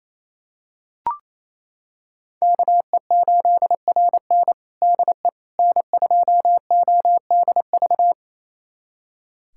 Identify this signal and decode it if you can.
Morse code, heard as a keyed tone, about 27 wpm, 700 Hz. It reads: KE8RN DE N2ODV
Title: USA Callsign DE Callsign - 5 Characters - 28wpm